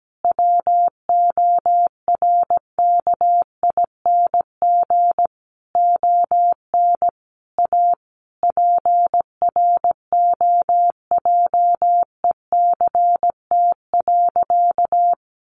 Morse code, heard as a keyed tone, about 17 wpm, 700 Hz. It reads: WORKING ON A PROJECT.
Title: Sentences from Top 1400 Words 17wpm